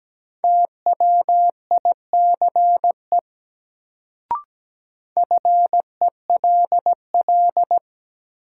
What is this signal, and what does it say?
Morse code, heard as a keyed tone, about 17 wpm, 700 Hz. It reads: TWICE FELL
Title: Sentences from Top 1600 Words - New Word Drill Alt - 17wpm